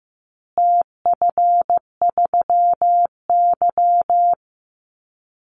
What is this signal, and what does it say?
Morse code, heard as a keyed tone, about 15 wpm, 700 Hz. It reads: TF3Y
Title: Call Signs Repeated 3x - 15wpm